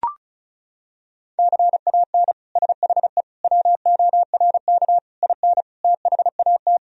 Morse code, heard as a keyed tone, about 36 wpm, 700 Hz. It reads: CAN SHE WORK IN THAT
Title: Sentences from Top 1000 Words - Review Part 4 - 35wpm